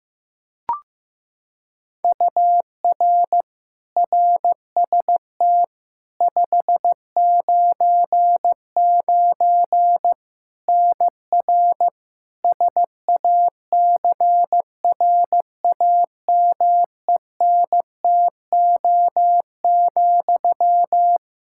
Morse code, heard as a keyed tone, about 16 wpm, 700 Hz. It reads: UR RST 599 NR SACRAMENTO,
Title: Practice QSO - Exchange 1 - Beginning - 15wpm